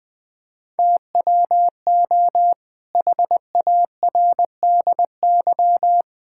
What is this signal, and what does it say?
Morse code, heard as a keyed tone, about 19 wpm, 700 Hz. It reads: TWO HARDY
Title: Pangram Challenge v2 - 20wpm